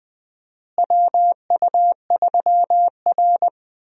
Morse code, heard as a keyed tone, about 20 wpm, 700 Hz. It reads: WU3R
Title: USA Call Signs - 4 Characters - Repeated 2x - 20wpm